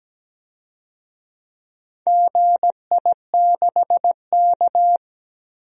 Morse code, heard as a keyed tone, about 17 wpm, 700 Hz. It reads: GI6K
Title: DXCC-100 Call Signs - 4 Characters - 17wpm